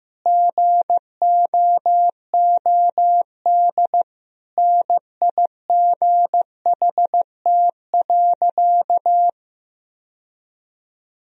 Morse code, heard as a keyed tone, about 15 wpm, 700 Hz. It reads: GOOD NIGHT.
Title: Sentences from Top 1000 Words - Review Part 4 - 15wpm